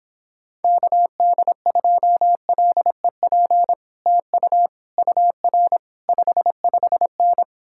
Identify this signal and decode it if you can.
Morse code, heard as a keyed tone, about 31 wpm, 700 Hz. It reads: KD2LEP TU UR 55N